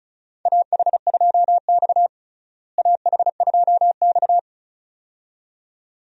Morse code, heard as a keyed tone, about 34 wpm, 700 Hz. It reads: AH2X AH2X
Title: USA Call Signs - 4 Characters - Repeated 2x - 35wpm